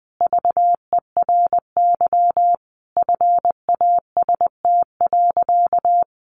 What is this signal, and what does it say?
Morse code, heard as a keyed tone, about 20 wpm, 700 Hz. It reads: VERY FAST.
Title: Sentences from Top 1100 Words 20wpm